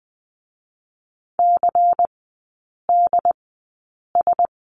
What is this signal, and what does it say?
Morse code, heard as a keyed tone, about 20 wpm, spaced as for 8 wpm, 690 Hz. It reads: CDS
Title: 093 Lesson 16e - Letter c - Three Characters 20wpm